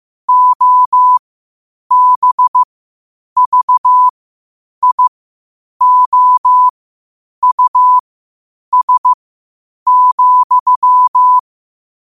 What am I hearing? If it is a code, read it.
Morse code, heard as a keyed tone, about 15 wpm, spaced as for 8 wpm, 1000 Hz. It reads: OBVIOUS,
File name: qotdfiles / 8wpm